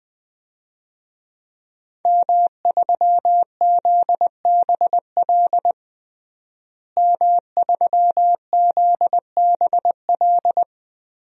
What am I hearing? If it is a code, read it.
Morse code, heard as a keyed tone, about 21 wpm, 700 Hz. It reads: M3ZBL M3ZBL
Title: DXCC-100 Call Signs - 5 Characters - Repeated 2x - 20wpm